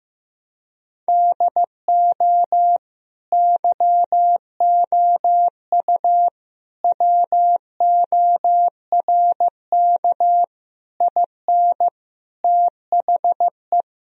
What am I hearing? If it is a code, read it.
Morse code, heard as a keyed tone, about 16 wpm, 700 Hz. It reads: DO YOU WORK IN THE
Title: Sentences from Top 1000 Words - Review Part 4 - 15wpm